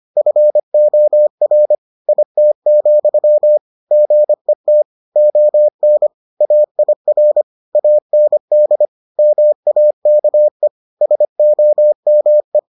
Morse code, heard as a keyed tone, about 25 wpm, 600 Hz. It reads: FOR IT, GET ON AIR AND MAKE SOME